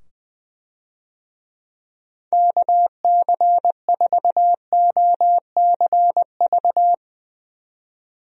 Morse code, heard as a keyed tone, about 20 wpm, 700 Hz. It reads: KC4OCV
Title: USA Call Signs - 6 Characters - 20wpm